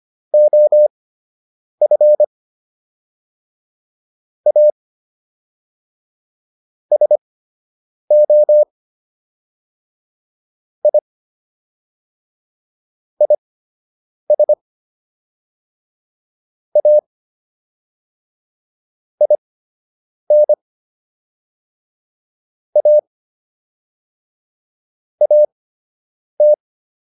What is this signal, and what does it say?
Morse code, heard as a keyed tone, about 25 wpm, spaced as for 8 wpm, 600 Hz. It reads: OF A SO I IS A IN A AT